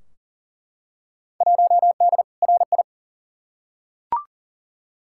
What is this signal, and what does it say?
Morse code, heard as a keyed tone, about 40 wpm, 700 Hz. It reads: 1D RI
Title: Field Day Exchanges - 40wpm